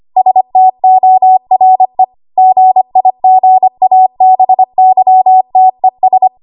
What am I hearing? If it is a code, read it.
Morse code, heard as a keyed tone, about 25 wpm, 750 Hz. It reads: STORE GIGABYTES
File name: Archive / 25 WPM / 25WPM